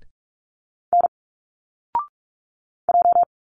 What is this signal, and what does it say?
Morse code, heard as a keyed tone, about 46 wpm, 700 Hz. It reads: N J
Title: Single Letters 45wpm